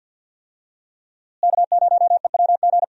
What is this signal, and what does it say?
Morse code, heard as a keyed tone, about 52 wpm, 700 Hz. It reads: K0EWG
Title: USA Call Signs - 5 Characters - Repeated 3x - 50wpm